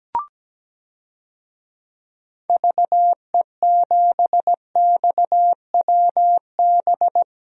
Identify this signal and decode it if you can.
Morse code, heard as a keyed tone, about 17 wpm, 700 Hz. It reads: VE7XWB
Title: DXCC-100 Call Signs - 6 Characters - 17wpm